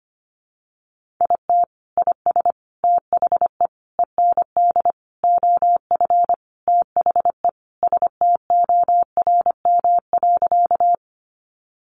Morse code, heard as a keyed tone, about 25 wpm, 700 Hz. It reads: IT IS THE END OF THE STORM.
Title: Sentences from Top 1500 Words 25wpm